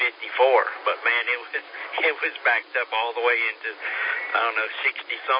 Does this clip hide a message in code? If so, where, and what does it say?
Yes: Emergency Alert System (SAME) tones from 1.0 to 4.2 s, heard as harsh buzzing bursts, with speech.